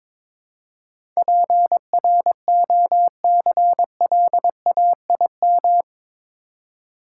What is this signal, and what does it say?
Morse code, heard as a keyed tone, about 22 wpm, 700 Hz. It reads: PROCLAIM
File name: practice / 22wpm / Eight Letter Words - 22wpm